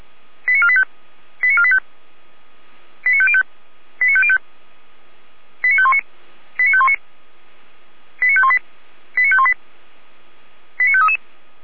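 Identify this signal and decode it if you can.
Five-tone selective call (ZVEI), heard as quick runs of five tones, tones of about 70 ms. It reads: 89485 89485 89505 89505 89519 89519 89518 89518 89533
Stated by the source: ZVEI five-tone call received by SDR.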